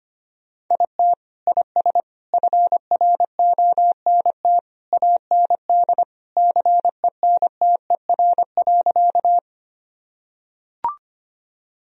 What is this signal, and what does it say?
Morse code, heard as a keyed tone, about 25 wpm, 700 Hz. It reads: IT IS FRONT AND CENTER.
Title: Sentences from Top 1000 Words - Review Part 2 - 25wpm